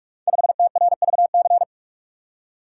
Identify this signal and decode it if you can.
Morse code, heard as a keyed tone, about 44 wpm, 700 Hz. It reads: 5TRVC
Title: Code Groups - 5 Letter:Numbers - 45wpm